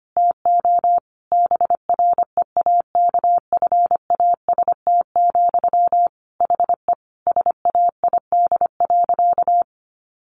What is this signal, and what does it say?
Morse code, heard as a keyed tone, about 25 wpm, 700 Hz. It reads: TO BREAKFAST, HE SAID.
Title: Sentences from Top 1500 Words 25wpm